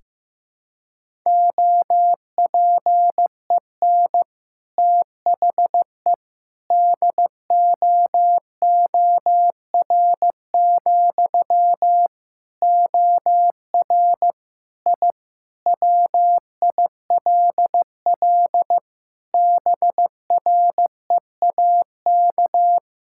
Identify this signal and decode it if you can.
Morse code, heard as a keyed tone, about 15 wpm, 700 Hz. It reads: OPEN THE DOOR, OR I WILL BREAK
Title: Sentences from Top 1000 Words - Review Part 6 - 15wpm